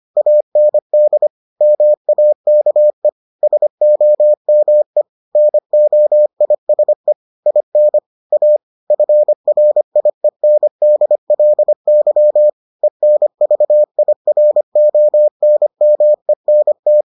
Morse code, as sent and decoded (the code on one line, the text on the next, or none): .- -. -.. / -- .- -.- . / ... --- -- . / -. --- .. ... / .. -. / .- / ..-. .-. .. -. -.. .-.. -.-- / . -. ...- .. .-. --- -. -- . -. -
AND MAKE SOME NOISE IN A FRIENDLY ENVIRONMENT